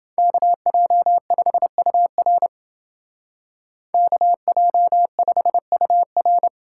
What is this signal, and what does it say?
Morse code, heard as a keyed tone, about 28 wpm, 700 Hz. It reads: KJ5UR KJ5UR